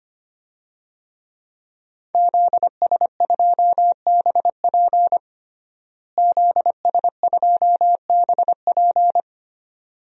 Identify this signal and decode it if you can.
Morse code, heard as a keyed tone, about 25 wpm, 700 Hz. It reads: ZS2BP ZS2BP
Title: DXCC-100 Call Signs - 5 Characters - Repeated 2x - 25wpm